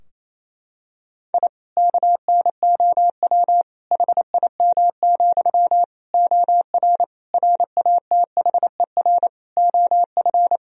Morse code, heard as a keyed tone, about 28 wpm, 700 Hz. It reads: I KNOW HIM, OR RATHER OF
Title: Sentences from Top 1000 Words - Review Part 4 - 28wpm